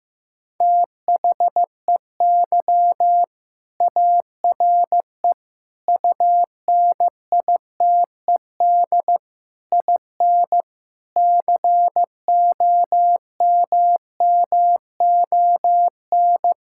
Morse code, heard as a keyed tone, about 15 wpm, 700 Hz. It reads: THEY ARE UNITED IN COMMON
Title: Sentences from Top 1000 Words - Review Part 5 - 15wpm